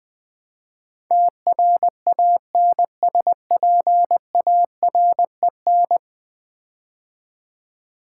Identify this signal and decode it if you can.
Morse code, heard as a keyed tone, about 20 wpm, 700 Hz. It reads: TRANSPAREN
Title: Build A Word - 5 Letters At A Time - 12-Letter Words - 20wpm